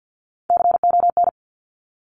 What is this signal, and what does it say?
Morse code, heard as a keyed tone, about 50 wpm, 700 Hz. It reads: CON
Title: Top Three Letter Combinations in Words 50wpm